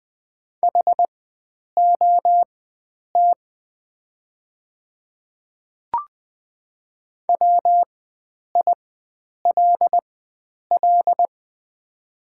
Morse code, heard as a keyed tone, about 20 wpm, spaced as for 9 wpm, 700 Hz. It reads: HOT WILL
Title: 4x Character Spacing - Top 100 Words - 20wpm